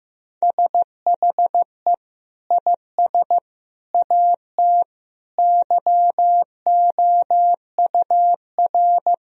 Morse code, heard as a keyed tone, about 15 wpm, 700 Hz. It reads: SHE IS AT YOUR
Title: Sentences from Top 1000 Words - Review Part 1 - 15wpm